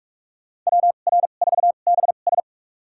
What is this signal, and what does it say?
Morse code, heard as a keyed tone, about 46 wpm, spaced as for 32 wpm, 700 Hz. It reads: WR4BS
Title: USA Call Signs - 5 Characters - 2x Character Spacing - 45wpm